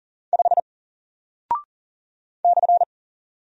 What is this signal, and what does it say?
Morse code, heard as a keyed tone, about 40 wpm, 700 Hz. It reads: 5 /
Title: Single Character (Letter:Number:Punctuation:Prosign) Alt 40wpm